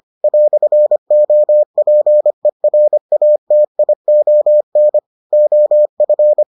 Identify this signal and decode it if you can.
Morse code, heard as a keyed tone, about 25 wpm, 600 Hz. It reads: "OPERATION OF